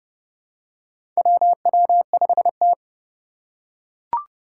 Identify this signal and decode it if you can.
Morse code, heard as a keyed tone, about 29 wpm, 700 Hz. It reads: WW5T
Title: USA Call Signs - 4 Characters - 3x Character Spacing - 30wpm